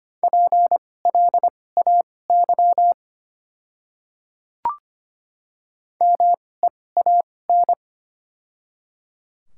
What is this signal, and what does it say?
Morse code, heard as a keyed tone, about 25 wpm, spaced as for 18 wpm, 700 Hz. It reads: PLAY MEAN
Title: 2x Character Spacing - Top 300 Words - 25wpm